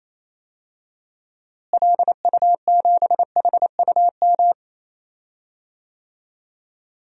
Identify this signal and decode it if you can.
Morse code, heard as a keyed tone, about 28 wpm, 700 Hz. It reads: LU7HUM
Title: DXCC-100 Call Signs - 6 Characters - Repeated 2x - 28wpm